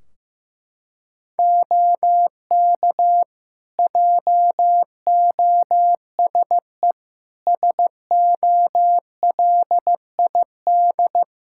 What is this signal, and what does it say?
Morse code, heard as a keyed tone, about 15 wpm, 700 Hz. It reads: OK JOSE SOLID
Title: Practice QSO - Exchange 3 - More Info - 15wpm